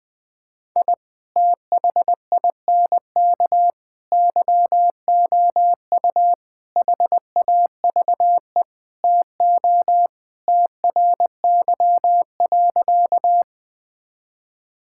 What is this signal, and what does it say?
Morse code, heard as a keyed tone, about 20 wpm, 700 Hz. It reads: I THINK YOU HAVE TO TRY.
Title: Sentences with 5-Word Phrases - 20wpm